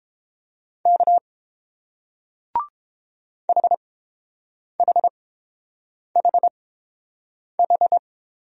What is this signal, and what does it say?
Morse code, heard as a keyed tone, about 30 wpm, spaced as for 7 wpm, 700 Hz. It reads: K HHHH